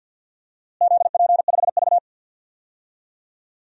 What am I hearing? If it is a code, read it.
Morse code, heard as a keyed tone, about 50 wpm, 700 Hz. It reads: ZP5V
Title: DXCC-100 Call Signs - 4 Characters - Repeated 3x - 50wpm